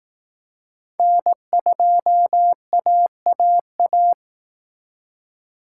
Morse code, heard as a keyed tone, about 18 wpm, 700 Hz. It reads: N2AAA
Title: Speed Racing - USA 1x3 Call Signs 15wpm